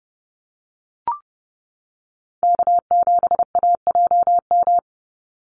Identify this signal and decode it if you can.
Morse code, heard as a keyed tone, about 30 wpm, 700 Hz. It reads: K7AJM
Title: Call Signs - 30wpm